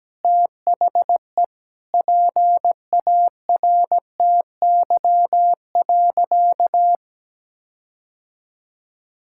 Morse code, heard as a keyed tone, about 17 wpm, 700 Hz. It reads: THE PARTY.
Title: Sentences from Top 1600 Words 17wpm